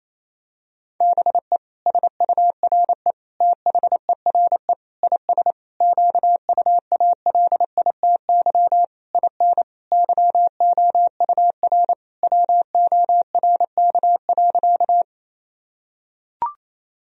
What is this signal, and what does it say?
Morse code, heard as a keyed tone, about 28 wpm, 700 Hz. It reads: BE SURE THERE IS QUALITY IN YOUR WORK.
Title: Sentences from Top 1000 Words - Review Part 4 - 28wpm